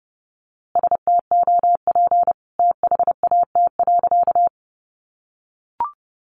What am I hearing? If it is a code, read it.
Morse code, heard as a keyed tone, about 30 wpm, 700 Hz. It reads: STOP THAT.
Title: A Sound Of Thunder 30wpm